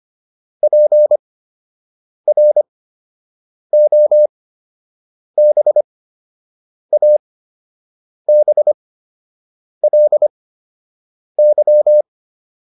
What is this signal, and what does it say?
Morse code, heard as a keyed tone, about 25 wpm, spaced as for 7 wpm, 600 Hz. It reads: PROBABLY